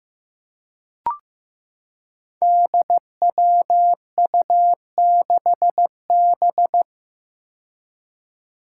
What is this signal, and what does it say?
Morse code, heard as a keyed tone, about 15 wpm, 700 Hz. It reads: DWU6B
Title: Five Challenging Letter-Number Combinations 15wpm